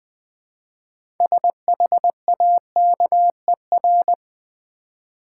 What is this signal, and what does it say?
Morse code, heard as a keyed tone, about 20 wpm, 700 Hz. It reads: SHAKER
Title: Easy Words with Hints - 20wpm